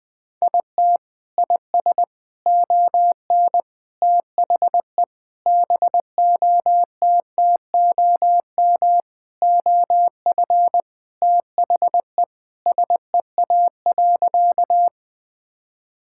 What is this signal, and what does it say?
Morse code, heard as a keyed tone, about 20 wpm, 700 Hz. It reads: IT IS ON THE BOTTOM OF THE SEA.
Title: Sentences from Top 1000 Words - Review Part 6 - 20wpm